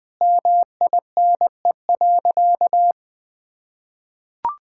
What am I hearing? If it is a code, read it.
Morse code, heard as a keyed tone, about 20 wpm, 700 Hz. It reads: MINE.
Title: Sentences with 7-Word Phrases - Spoken Beginning - 20wpm